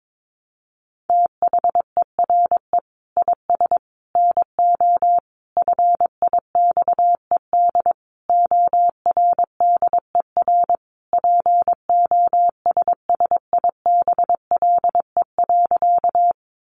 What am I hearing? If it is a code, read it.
Morse code, heard as a keyed tone, about 22 wpm, 700 Hz. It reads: THERE IS NO FIXED ORDER POSSIBLE.
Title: Sentences from Top 1100 Words 22wpm